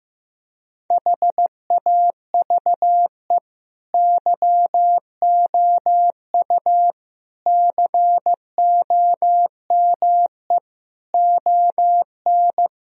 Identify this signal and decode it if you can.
Morse code, heard as a keyed tone, about 16 wpm, 700 Hz. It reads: HAVE YOU COME ON
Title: Sentences from Top 1000 Words - Review Part 3 - 15wpm